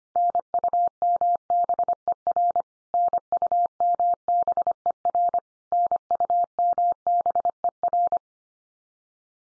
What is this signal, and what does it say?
Morse code, heard as a keyed tone, about 25 wpm, 700 Hz. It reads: NUMBER NUMBER NUMBER
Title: Top 100 Words 25wpm x3